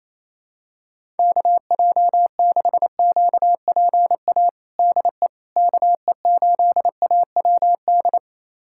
Morse code, heard as a keyed tone, about 27 wpm, 700 Hz. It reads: KJ6QPA DE KE8AWD
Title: USA Callsign DE Callsign - 6 Characters - 28wpm